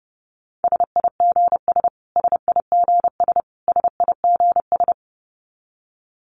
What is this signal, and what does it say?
Morse code, heard as a keyed tone, about 30 wpm, 700 Hz. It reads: SIGS SIGS SIGS
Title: Abbreviations Repeated 3x - 30wpm